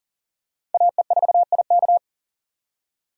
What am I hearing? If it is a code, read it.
Morse code, heard as a keyed tone, about 40 wpm, 700 Hz. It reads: AE4IK
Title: USA Call Signs - 5 Characters - 40wpm